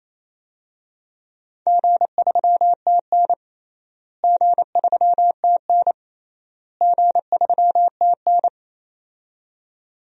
Morse code, heard as a keyed tone, about 28 wpm, 700 Hz. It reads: G3TN G3TN G3TN
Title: DXCC-100 Call Signs - 4 Characters - Repeated 3x - 28wpm